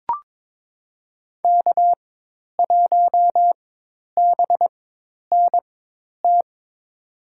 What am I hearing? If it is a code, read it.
Morse code, heard as a keyed tone, about 23 wpm, spaced as for 10 wpm, 700 Hz. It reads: K1BNT
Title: USA Call Signs - 5 Characters - 4x Character Spacing - 22wpm